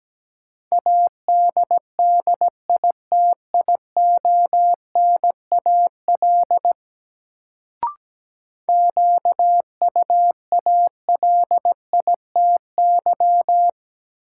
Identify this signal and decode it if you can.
Morse code, heard as a keyed tone, about 17 wpm, 700 Hz. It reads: ADDITIONAL QUALITY
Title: Sentences from Top 1000 Words - Review Part 4 - New Word Drill - 17wpm